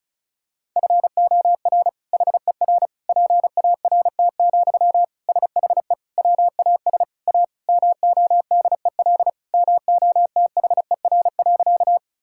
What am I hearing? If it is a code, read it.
Morse code, heard as a keyed tone, about 36 wpm, 700 Hz. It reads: FOR HER PART, SHE WAS A MODEL MOTHER.
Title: Sentences from Top 1000 Words - Review Part 5 - 35wpm